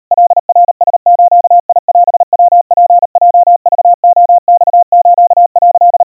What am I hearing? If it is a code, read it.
Morse code (Japanese Wabun code), heard as a keyed tone, about 38 wpm, 700 Hz. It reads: カナラズオヤツヲクレマス」
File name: kawa / Morse sample / kamyu 140 cpm 700 Hz Crispy